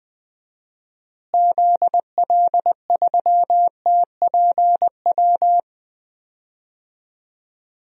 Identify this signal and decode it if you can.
Morse code, heard as a keyed tone, about 20 wpm, 700 Hz. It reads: ZL3TPW
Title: DXCC-100 Call Signs - 6 Characters - Repeated 3x - 20wpm